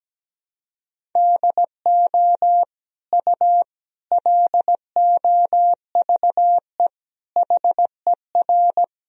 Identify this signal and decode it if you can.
Morse code, heard as a keyed tone, about 17 wpm, 700 Hz. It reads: DO U LOVE HER
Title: Sentences from Top 400 Words and Abbreviations 17wpm